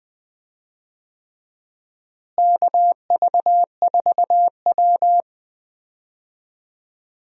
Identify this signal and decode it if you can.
Morse code, heard as a keyed tone, about 20 wpm, 700 Hz. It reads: KV4W
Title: USA Call Signs - 4 Characters - Repeated 3x - 20wpm